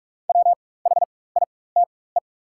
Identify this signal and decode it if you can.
Morse code, heard as a keyed tone, about 44 wpm, spaced as for 21 wpm, 690 Hz. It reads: WHITE
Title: 4x Character Spacing - Top 500 Words - 45wpm